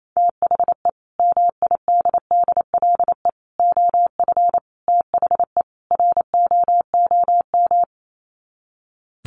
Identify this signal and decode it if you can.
Morse code, heard as a keyed tone, about 28 wpm, 700 Hz. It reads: THE MIDDLE OF THE ROOM
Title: Sets of 5 Words Encore2 28wpm